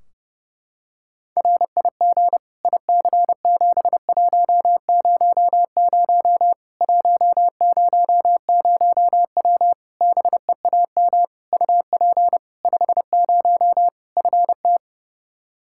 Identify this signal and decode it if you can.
Morse code, heard as a keyed tone, about 32 wpm, 700 Hz. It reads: RIG IC7100 100W BEAM UP 50 FT
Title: Practice QSO - Exchange 2 - Info - 30wpm